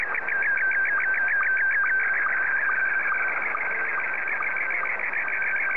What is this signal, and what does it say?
SSTV (slow-scan television), Scottie 1.